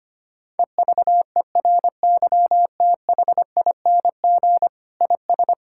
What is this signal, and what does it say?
Morse code, heard as a keyed tone, about 25 wpm, 700 Hz. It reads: EVERYTHING IS
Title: Sentences from Top 1000 Words - Review Part 6 - 25wpm